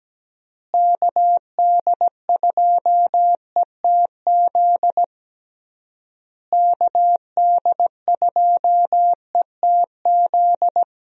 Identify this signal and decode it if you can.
Morse code, heard as a keyed tone, about 17 wpm, 700 Hz. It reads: KD2ETZ KD2ETZ
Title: USA Call Signs - 6 Characters - Repeated 3x - 17wpm